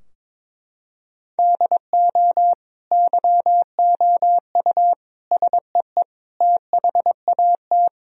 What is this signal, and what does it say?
Morse code, heard as a keyed tone, about 22 wpm, 700 Hz. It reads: DO YOU SEE THAT
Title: Sentences from Top 1000 Words - Review Part 3 - 22wpm